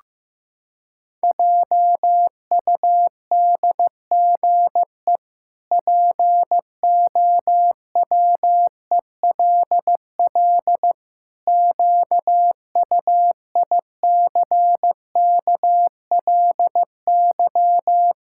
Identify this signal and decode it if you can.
Morse code, heard as a keyed tone, about 15 wpm, 700 Hz. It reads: JUDGE POWELL QUICKLY
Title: Pangram Challenge v2 - 15wpm